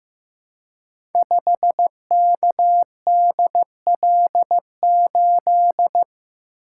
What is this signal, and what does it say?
Morse code, heard as a keyed tone, about 15 wpm, 700 Hz. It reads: 5KDL8
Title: Code Groups - 5 Letter:Numbers - 15wpm